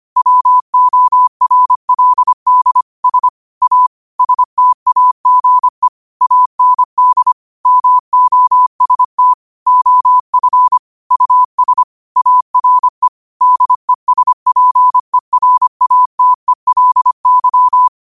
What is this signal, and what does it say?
Morse code, heard as a keyed tone, about 25 wpm, 1000 Hz. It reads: WORLD S A STAGE AND MOST OF US ARE DESPERATELY